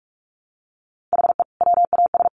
Morse code, heard as a keyed tone, about 44 wpm, 700 Hz. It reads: HE WAS
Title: Sentences from Top 1200 Words 45wpm